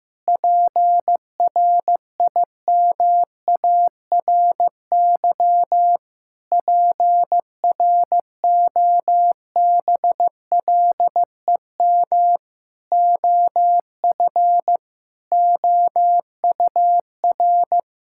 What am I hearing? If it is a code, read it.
Morse code, heard as a keyed tone, about 15 wpm, 700 Hz. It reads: PRIMARY PROBLEM OF OUR
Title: Sentences from Top 1000 Words - Review Part 5 - 15wpm